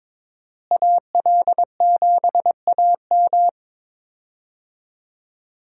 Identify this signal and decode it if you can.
Morse code, heard as a keyed tone, about 22 wpm, 700 Hz. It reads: AL7AM
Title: USA Call Signs - 5 Characters - Repeated 2x - 22wpm